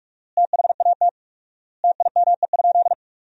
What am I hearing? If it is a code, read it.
Morse code, heard as a keyed tone, about 46 wpm, 700 Hz. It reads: THAT TIME?
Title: 4x Word-Spacing - Sentences from Top 200 Words - 45wpm